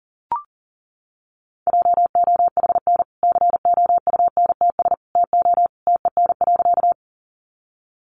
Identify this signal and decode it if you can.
Morse code, heard as a keyed tone, about 40 wpm, 700 Hz. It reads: JOHN COUNTS TO TEN.
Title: Easy Phrases 40wpm